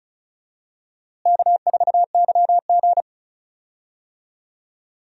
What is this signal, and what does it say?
Morse code, heard as a keyed tone, about 34 wpm, 700 Hz. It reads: K4YG
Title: USA Call Signs - 4 Characters - Repeated 2x - 35wpm